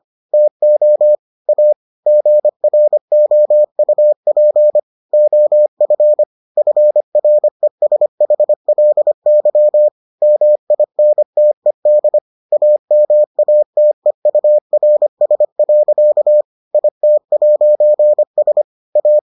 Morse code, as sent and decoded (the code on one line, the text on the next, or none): - --- / .- / --. .-. --- ..- .--. / --- ..-. / ..-. .-. ... .... .-.. -.-- / -- .. -. - . -.. / .- -- .- - . ..- .-. ... .-.-.- / .. - .----. ... / .-
TO A GROUP OF FRESHLY MINTED AMATEURS. IT'S A